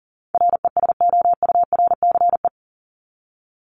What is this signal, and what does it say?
Morse code, heard as a keyed tone, about 40 wpm, 700 Hz. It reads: RESOURCE
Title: Top 1001-1250 Words 40wpm